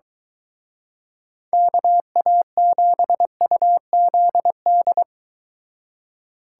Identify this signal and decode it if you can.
Morse code, heard as a keyed tone, about 23 wpm, 700 Hz. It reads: KA7UZD